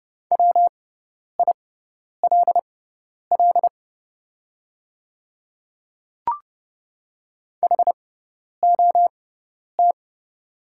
Morse code, heard as a keyed tone, about 30 wpm, spaced as for 10 wpm, 690 Hz. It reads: WILL HOT
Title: 6x Character Spacing - Top 200 Words - 30wpm